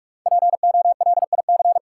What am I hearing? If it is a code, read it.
Morse code, heard as a keyed tone, about 41 wpm, 700 Hz. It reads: POLIC
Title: Sentences from Top 1000 Words - Review Part 5 - New Word Drill - 45wpm